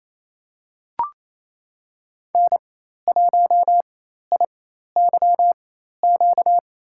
Morse code, heard as a keyed tone, about 28 wpm, spaced as for 13 wpm, 700 Hz. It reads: N1IYQ
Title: USA Call Signs - 4x Character Spacing - 28wpm